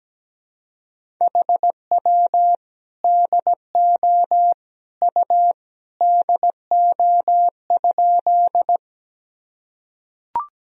Morse code, heard as a keyed tone, about 17 wpm, 700 Hz. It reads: HW DO U DO?
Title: Sentences from Top 100 Words and Abbreviations 17wpm